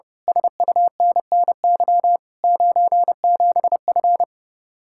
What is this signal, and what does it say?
Morse code, heard as a keyed tone, about 28 wpm, 700 Hz. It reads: SUNNY 97F
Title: Speed-Racing Common QSO Phrases Encore - 30wpm